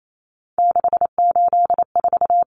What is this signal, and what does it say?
Morse code, heard as a keyed tone, about 28 wpm, 700 Hz. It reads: 684
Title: Numbers - 3-Digit 28wpm